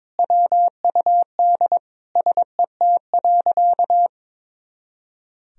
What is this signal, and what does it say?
Morse code, heard as a keyed tone, about 22 wpm, 700 Hz. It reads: WUD SET.
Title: Sentences from Top 400 Words and Abbreviations 22wpm